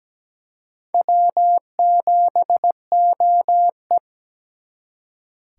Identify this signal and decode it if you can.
Morse code, heard as a keyed tone, about 20 wpm, 700 Hz. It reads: W7OE
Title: Speed Racing - USA 1x2 Call Signs 17wpm